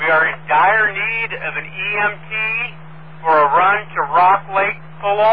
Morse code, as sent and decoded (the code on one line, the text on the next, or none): none